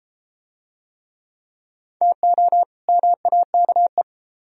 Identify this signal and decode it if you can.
Morse code, heard as a keyed tone, about 31 wpm, 700 Hz. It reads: TO MAKE